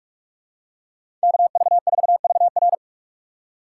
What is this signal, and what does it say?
Morse code, heard as a keyed tone, about 42 wpm, 700 Hz. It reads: KV4VR
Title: USA Call Signs - 5 Characters - 2x Character Spacing - 45wpm